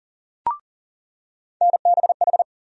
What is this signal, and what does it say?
Morse code, heard as a keyed tone, about 40 wpm, 700 Hz. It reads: NBH
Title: USA Call Sign Postfixes - 3 Letters 40wpm